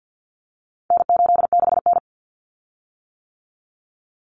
Morse code, heard as a keyed tone, about 50 wpm, 700 Hz. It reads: N86N
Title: Common Cut Numbers - 4-Digit 50wpm